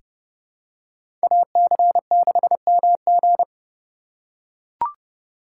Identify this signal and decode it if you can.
Morse code, heard as a keyed tone, about 32 wpm, 700 Hz. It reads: AC6MG
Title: USA Call Signs - 5 Characters - 2x Character Spacing - 30wpm